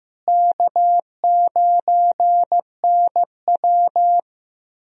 Morse code, heard as a keyed tone, about 15 wpm, 700 Hz. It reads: K9NW
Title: USA Call Signs - 4 Characters - Repeated 2x - 15wpm